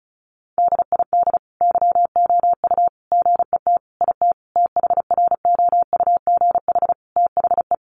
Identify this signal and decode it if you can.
Morse code, heard as a keyed tone, about 34 wpm, 700 Hz. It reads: DID YOU GET IT THROUGH THE
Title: Sentences from Top 1300 Words 35wpm